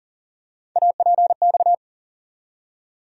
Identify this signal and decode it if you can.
Morse code, heard as a keyed tone, about 40 wpm, 700 Hz. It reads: APX
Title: USA Call Sign Postfixes - 3 Letters 40wpm